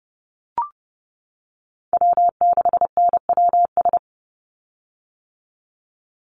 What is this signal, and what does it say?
Morse code, heard as a keyed tone, about 30 wpm, 700 Hz. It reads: W6NWS
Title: Call Signs - 30wpm